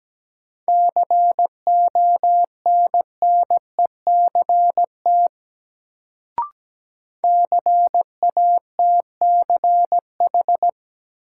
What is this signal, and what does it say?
Morse code, heard as a keyed tone, about 17 wpm, 700 Hz. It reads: CONNECT CATCH
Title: Sentences from Top 1500 Words - New Word Drill Alt - 17wpm